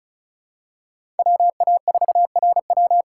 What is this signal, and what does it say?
Morse code, heard as a keyed tone, about 34 wpm, 700 Hz. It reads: WA4RW
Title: USA Call Signs - 5 Characters - 4x Character Spacing - 35wpm